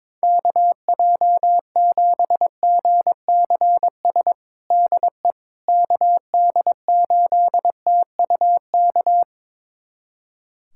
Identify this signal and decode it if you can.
Morse code, heard as a keyed tone, about 22 wpm, 700 Hz. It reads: KJ7GCS DE KD8TUK
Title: USA Callsign DE Callsign 22wpm